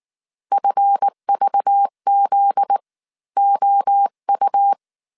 Morse code, heard as a keyed tone, about 19 wpm, spaced as for 15 wpm, 800 Hz. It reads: FVZOU
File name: CW / DET / 1 9 1500C S B 18wpm 2551